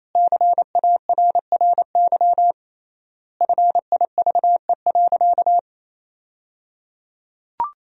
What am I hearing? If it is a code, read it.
Morse code, heard as a keyed tone, about 28 wpm, 700 Hz. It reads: CARRY FIVE.
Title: 3x Word-Spacing - Sentences from Top 400 Words - 28wpm